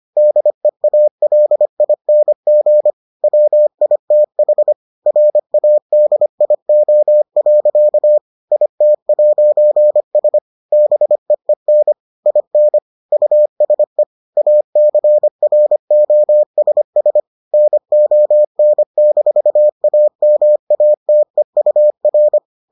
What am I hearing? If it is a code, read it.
Morse code, heard as a keyed tone, about 25 wpm, 600 Hz. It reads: DEALING WITH RADIO. IT'S BEEN IN USE ACROSS NON-AMATEUR